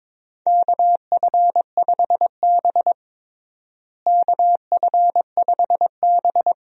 Morse code, heard as a keyed tone, about 22 wpm, 700 Hz. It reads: KF5B KF5B
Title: USA Call Signs - 4 Characters - Repeated 2x - 22wpm